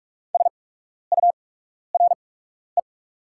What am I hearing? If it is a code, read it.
Morse code, heard as a keyed tone, about 44 wpm, spaced as for 12 wpm, 690 Hz. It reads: SURE
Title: 8x Character Spacing - Top 300 Words - 45wpm